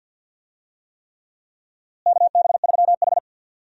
Morse code, heard as a keyed tone, about 52 wpm, 700 Hz. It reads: KB3H
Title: USA Call Signs - 4 Characters - Repeated 3x - 50wpm